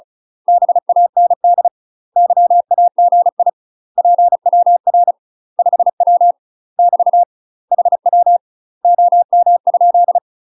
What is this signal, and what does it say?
Morse code, heard as a keyed tone, about 36 wpm, 700 Hz. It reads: BAND YAGI PWR 5W <BT> HW OM?
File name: UK MASTER QSOs TRAINING / 35WPM / UK MASTER QSOs 0032